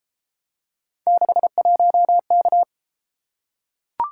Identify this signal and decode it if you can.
Morse code, heard as a keyed tone, about 30 wpm, 700 Hz. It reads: 61K